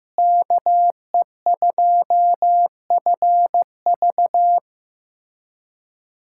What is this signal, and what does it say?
Morse code, heard as a keyed tone, about 15 wpm, 700 Hz. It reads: KE2FV
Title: USA Callsign DE Callsign - 5 Characters - 15wpm